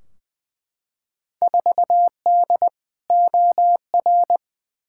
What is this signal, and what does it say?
Morse code, heard as a keyed tone, about 20 wpm, 700 Hz. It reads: 4D OR
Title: Field Day Exchanges Encore - 20wpm